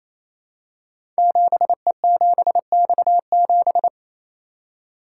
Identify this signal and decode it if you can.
Morse code, heard as a keyed tone, about 27 wpm, 700 Hz. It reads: 7E7X7
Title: Code Groups - 5 Letter:Numbers - 28wpm